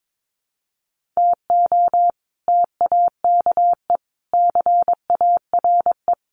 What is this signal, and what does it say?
Morse code, heard as a keyed tone, about 22 wpm, 700 Hz. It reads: TO TAKE CARE
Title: Sets of 4 Words 22wpm